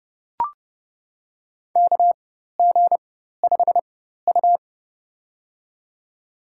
Morse code, heard as a keyed tone, about 30 wpm, spaced as for 14 wpm, 700 Hz. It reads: KG5U
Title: USA Call Signs - 4 Characters - 4x Character Spacing - 30wpm